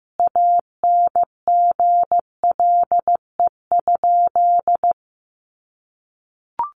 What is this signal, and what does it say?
Morse code, heard as a keyed tone, about 15 wpm, 700 Hz. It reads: ANGLE?
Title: Sentences from Top 1500 Words 15wpm